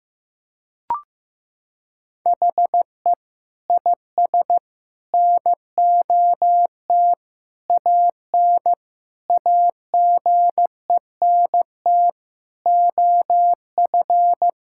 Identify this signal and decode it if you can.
Morse code, heard as a keyed tone, about 15 wpm, 700 Hz. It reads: HE IS NOT AN AGENT OF
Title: Sentences from Top 1600 Words 15wpm